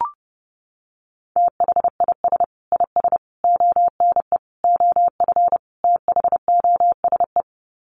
Morse code, heard as a keyed tone, about 30 wpm, 700 Hz. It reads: THIS IS ONE OF THOSE
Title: Sets of 5 Words Encore2 30wpm